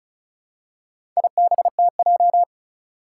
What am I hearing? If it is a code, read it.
Morse code, heard as a keyed tone, about 34 wpm, 700 Hz. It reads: IBTJ
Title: Four Challenging Letter-Number Combinations 35wpm